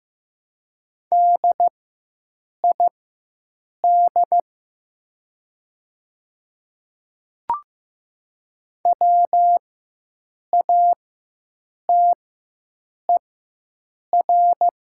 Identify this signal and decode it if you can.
Morse code, heard as a keyed tone, about 15 wpm, spaced as for 7 wpm, 700 Hz. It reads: DID WATER
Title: 4x Character Spacing - Top 100 Words - 15wpm